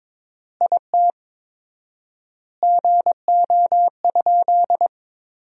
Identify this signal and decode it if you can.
Morse code, heard as a keyed tone, about 22 wpm, 700 Hz. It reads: IT GO?
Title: 4x Word-Spacing - Sentences from Top 200 Words - 22wpm